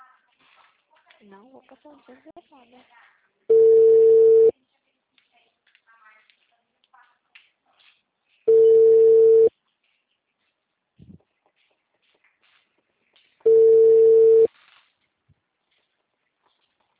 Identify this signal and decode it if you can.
Ringback tone (China, 450 Hz), with speech.
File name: out- -2014- - - .34977